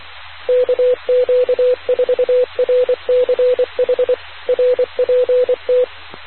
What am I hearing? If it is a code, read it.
Morse code, heard as a keyed tone, about 24 wpm, 500 Hz. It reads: KQ4RCH RPT